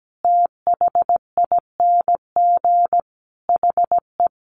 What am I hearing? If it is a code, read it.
Morse code, heard as a keyed tone, about 17 wpm, 700 Hz. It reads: THING HE
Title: Sentences from Top 300 Words 17wpm